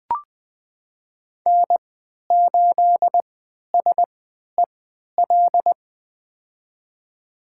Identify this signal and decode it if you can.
Morse code, heard as a keyed tone, about 20 wpm, spaced as for 11 wpm, 700 Hz. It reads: N8SEL
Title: USA Call Signs - 5 Characters - 3x Character Spacing - 20wpm